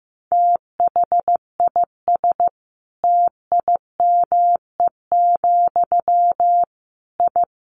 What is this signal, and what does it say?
Morse code, heard as a keyed tone, about 15 wpm, 700 Hz. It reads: THIS TIME, I
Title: A Sound Of Thunder 15wpm